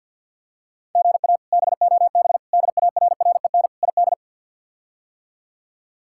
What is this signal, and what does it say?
Morse code, heard as a keyed tone, about 52 wpm, 700 Hz. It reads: GA BOB DARREN ID
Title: K1USN SST QSO Practice - 50wpm